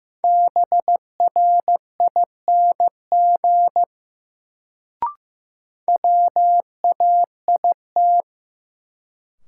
Morse code, heard as a keyed tone, about 15 wpm, 700 Hz. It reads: BRING WAIT
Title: Sets of 5 Words Encore - New Word Drill - 15wpm